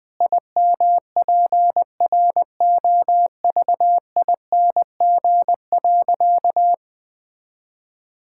Morse code, heard as a keyed tone, about 21 wpm, 700 Hz. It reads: IMPROVING.
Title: Sentences with 8-Word Phrases - Spoken Beginning - 20wpm